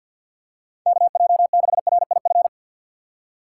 Morse code, heard as a keyed tone, about 49 wpm, 700 Hz. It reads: KJ6RIF
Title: USA Call Signs - 6 Characters - 6x Character Spacing - 50wpm